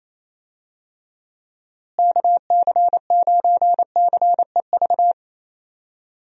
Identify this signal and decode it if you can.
Morse code, heard as a keyed tone, about 28 wpm, 700 Hz. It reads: KC9CEV
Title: USA Call Signs - 6 Characters - Repeated 2x - 28wpm